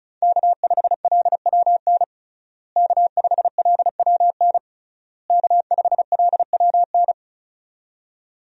Morse code, heard as a keyed tone, about 36 wpm, 700 Hz. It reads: K5LWN K5LWN K5LWN
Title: USA Call Signs - 5 Characters - Repeated 3x - 35wpm